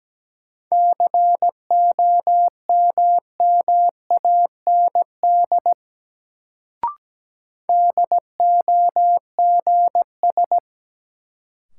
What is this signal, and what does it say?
Morse code, heard as a keyed tone, about 17 wpm, 700 Hz. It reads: COMMAND DOGS
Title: Sentences from Top 1600 Words - New Word Drill Alt - 17wpm